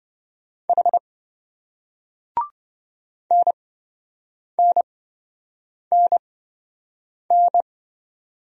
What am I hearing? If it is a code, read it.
Morse code, heard as a keyed tone, about 26 wpm, spaced as for 7 wpm, 700 Hz. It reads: H NNNN